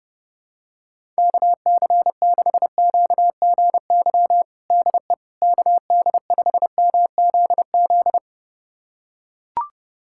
Morse code, heard as a keyed tone, about 32 wpm, 700 Hz. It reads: KC6QGY DE KD5MZZ
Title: USA Callsign DE Callsign - 6 Characters - 30wpm